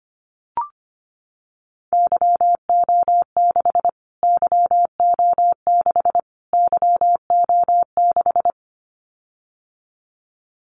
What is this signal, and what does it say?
Morse code, heard as a keyed tone, about 25 wpm, 700 Hz. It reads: YO6 YO6 YO6
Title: Callsign Prefixes for Top 25 DXCC - Repeated 3x - 25wpm